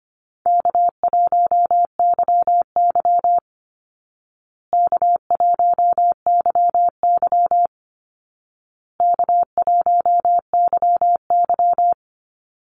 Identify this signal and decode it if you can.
Morse code, heard as a keyed tone, about 25 wpm, 700 Hz. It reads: K1YY K1YY K1YY
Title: Easy Call Signs Repeated 3x - 25wpm